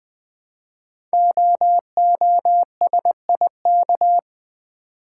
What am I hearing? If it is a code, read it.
Morse code, heard as a keyed tone, about 20 wpm, 700 Hz. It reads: OOSIK
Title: Code Groups - 5 Letters - 20wpm